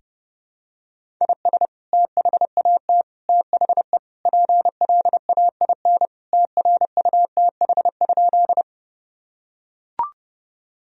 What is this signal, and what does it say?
Morse code, heard as a keyed tone, about 30 wpm, 700 Hz. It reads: IS THAT THE PLAIN TRUTH?
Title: Sentences from Top 1000 Words - Review Part 2 - 30wpm